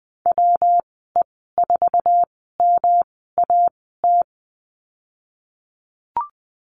Morse code, heard as a keyed tone, about 19 wpm, spaced as for 14 wpm, 700 Hz. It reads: WE4MAT
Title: 193 Lesson 31g - Letter v - Call Signs 20wpm